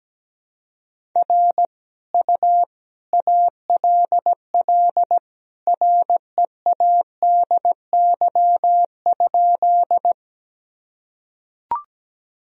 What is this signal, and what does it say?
Morse code, heard as a keyed tone, about 17 wpm, 700 Hz. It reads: R U ALL READY?
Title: Sentences from Top 400 Words and Abbreviations 17wpm